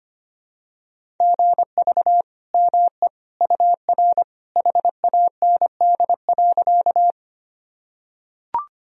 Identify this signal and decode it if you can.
Morse code, heard as a keyed tone, about 25 wpm, 700 Hz. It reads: GV ME UR HAND.
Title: Sentences from Top 200 Words and Abbreviations 25wpm